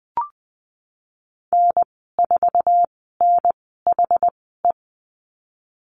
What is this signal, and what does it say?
Morse code, heard as a keyed tone, about 20 wpm, spaced as for 15 wpm, 700 Hz. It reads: N4NHE
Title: 069 Lesson 12g - Letter d - Call Signs 20wpm